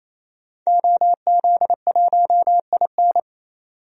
Morse code, heard as a keyed tone, about 28 wpm, 700 Hz. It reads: OZ1IN
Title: DXCC-100 Call Signs - 5 Characters - 28wpm